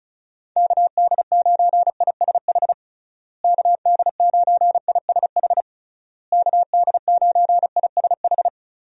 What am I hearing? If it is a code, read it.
Morse code, heard as a keyed tone, about 36 wpm, 700 Hz. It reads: KD9ISH KD9ISH KD9ISH
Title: USA Call Signs - 6 Characters - Repeated 3x - 35wpm